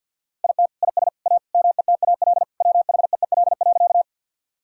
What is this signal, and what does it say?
Morse code, heard as a keyed tone, about 50 wpm, 700 Hz. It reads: IT IS A METAL WHEEL.
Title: Sentences from Top 1000 Words - Review Part 6 - 50wpm